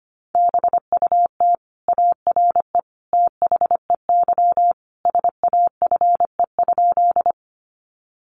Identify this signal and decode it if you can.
Morse code, heard as a keyed tone, about 25 wpm, 700 Hz. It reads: BUT ARE THEY SAFE?
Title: Sentences from Top 900 Words 25wpm